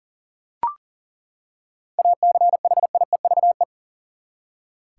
Morse code, heard as a keyed tone, about 40 wpm, 700 Hz. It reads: ACHIEVE
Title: Seven Letter Words - 40wpm